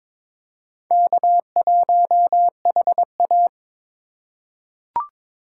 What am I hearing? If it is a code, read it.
Morse code, heard as a keyed tone, about 21 wpm, 700 Hz. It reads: K1HA
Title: USA Call Signs - 4 Characters - 4x Character Spacing - 22wpm